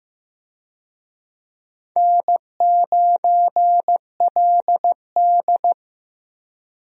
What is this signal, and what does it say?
Morse code, heard as a keyed tone, about 15 wpm, 700 Hz. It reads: N9LD
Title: USA Call Signs - 4 Characters - Repeated 3x - 15wpm